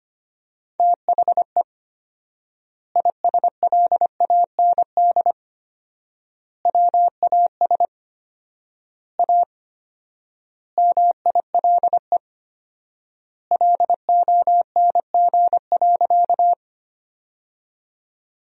Morse code, heard as a keyed tone, about 25 wpm, 700 Hz. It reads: THE ISLAND WAS A MILE LONG.
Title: 4x Word-Spacing - Sentences from Top 500 Words - 25wpm